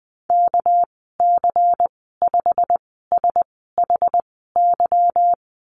Morse code, heard as a keyed tone, about 20 wpm, spaced as for 14 wpm, 700 Hz. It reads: KC5SHY
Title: 213 Lesson 34g - Number 8 - Call Signs 20wpm